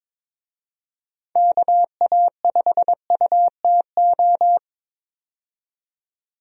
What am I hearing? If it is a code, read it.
Morse code, heard as a keyed tone, about 22 wpm, 700 Hz. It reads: KA5UTO
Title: USA Call Signs - 6 Characters - 22wpm